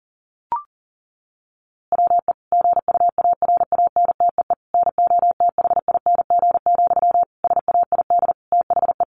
Morse code, heard as a keyed tone, about 40 wpm, 700 Hz. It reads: WE GUARANTEE NOTHING, SAID THE
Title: A Sound Of Thunder 40wpm